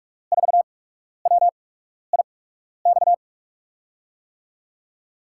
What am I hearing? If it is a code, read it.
Morse code, heard as a keyed tone, about 45 wpm, spaced as for 12 wpm, 700 Hz. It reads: 4WIX
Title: USA Call Signs - 6 Characters - 8x Character Spacing - 45wpm